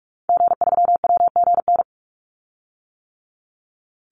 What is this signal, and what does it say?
Morse code, heard as a keyed tone, about 44 wpm, 700 Hz. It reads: G3WGN
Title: Call Signs - 45wpm